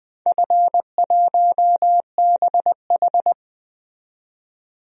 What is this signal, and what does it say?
Morse code, heard as a keyed tone, about 20 wpm, 700 Hz. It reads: F1BH
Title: DXCC-100 Call Signs - 4 Characters - 20wpm